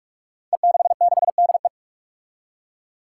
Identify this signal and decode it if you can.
Morse code, heard as a keyed tone, about 42 wpm, 700 Hz. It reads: E66DE
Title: Five Challenging Letter-Number Combinations 45wpm